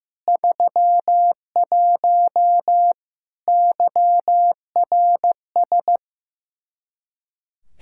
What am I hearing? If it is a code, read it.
Morse code, heard as a keyed tone, about 16 wpm, 700 Hz. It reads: 31 YRS
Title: Practice QSO - Exchange 3 - More Info - 15wpm